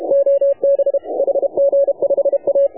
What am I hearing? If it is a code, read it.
Morse code, heard as a keyed tone, about 32 wpm, 530 Hz. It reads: OB HG5A
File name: AUDIO2019 / HA7UI CW